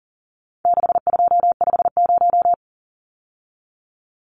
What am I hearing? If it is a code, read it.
Morse code, heard as a keyed tone, about 40 wpm, 700 Hz. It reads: 6250
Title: Numbers - 4-Digit 40wpm